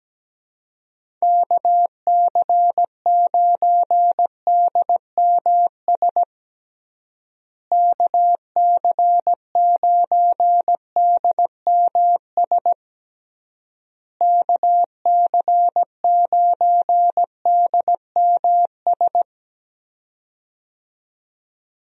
Morse code, heard as a keyed tone, about 17 wpm, 700 Hz. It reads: KC9DMS KC9DMS KC9DMS
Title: USA Call Signs - 6 Characters - Repeated 3x - 17wpm